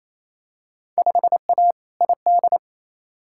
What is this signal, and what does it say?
Morse code, heard as a keyed tone, about 28 wpm, 700 Hz. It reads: 5A ID
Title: Field Day Exchanges Encore - 28wpm